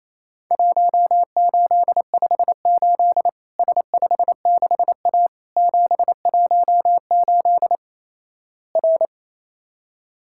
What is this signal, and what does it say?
Morse code, heard as a keyed tone, about 28 wpm, 700 Hz. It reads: 1858 S56A 718 R
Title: Worked All Europe DX Contest - QTC Exchange - 28wpm